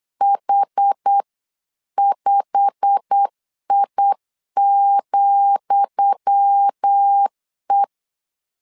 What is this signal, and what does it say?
Morse code, heard as a keyed tone, about 8 wpm, 800 Hz. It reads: H5I,E
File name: LEZIONI / 10 SIGNS / 10D COMA 8wpm 3749